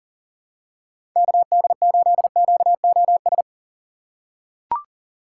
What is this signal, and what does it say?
Morse code, heard as a keyed tone, about 40 wpm, 700 Hz. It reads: KD8QOS
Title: USA Call Signs - 6 Characters - 3x Character Spacing - 40wpm